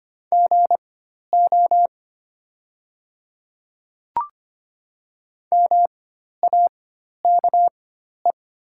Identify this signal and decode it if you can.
Morse code, heard as a keyed tone, about 25 wpm, spaced as for 12 wpm, 690 Hz. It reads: GO MAKE
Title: 4x Character Spacing - Top 200 Words - 25wpm